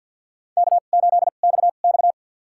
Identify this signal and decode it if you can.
Morse code, heard as a keyed tone, about 48 wpm, spaced as for 35 wpm, 700 Hz. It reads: K8XX
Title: USA Call Signs - 4 Characters - 2x Character Spacing - 50wpm